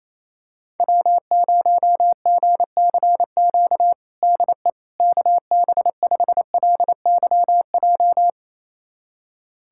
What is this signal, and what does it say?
Morse code, heard as a keyed tone, about 28 wpm, 700 Hz. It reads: W0GCQ DE KB5LYJ
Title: USA Callsign DE Callsign 28wpm